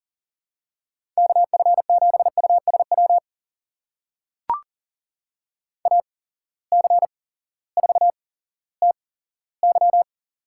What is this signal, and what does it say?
Morse code, heard as a keyed tone, about 40 wpm, spaced as for 18 wpm, 700 Hz. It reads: KF7USW AC4TY
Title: USA Call Signs - 8x Character Spacing - 40wpm